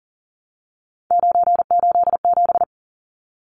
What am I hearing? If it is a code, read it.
Morse code, heard as a keyed tone, about 40 wpm, 700 Hz. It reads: 987
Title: Numbers - 3-Digit 40wpm